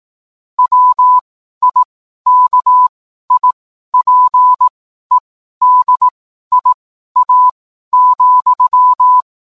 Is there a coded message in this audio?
Yes: Morse code from 0.6 to 9.2 s, heard as a keyed tone, about 18 wpm, spaced as for 12 wpm, 1000 Hz. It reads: WIKIPEDIA,